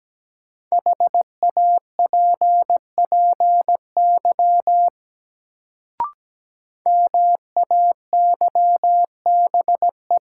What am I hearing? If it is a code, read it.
Morse code, heard as a keyed tone, about 17 wpm, 700 Hz. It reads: HAPPY MAYBE
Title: Sentences from Top 1000 Words - New Word Drill - 17wpm